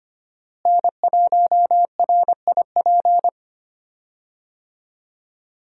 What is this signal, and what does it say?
Morse code, heard as a keyed tone, about 25 wpm, 700 Hz. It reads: N1RIP
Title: USA Call Signs - 5 Characters - Repeated 3x - 25wpm